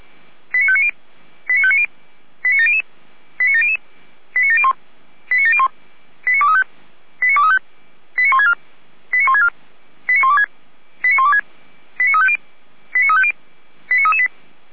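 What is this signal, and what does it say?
Five-tone selective call (ZVEI), heard as quick runs of five tones, tones of about 70 ms. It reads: 89590 89590 89700 89700 89701 89701 89235 89235 89164 89164 89126 89126 89350 89350 89338
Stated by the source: ZVEI five-tone selective call 89115, received on an SDR.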